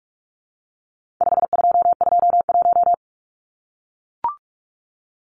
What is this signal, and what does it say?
Morse code, heard as a keyed tone, about 44 wpm, 700 Hz. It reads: 5221
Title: Numbers - 4-Digit 45wpm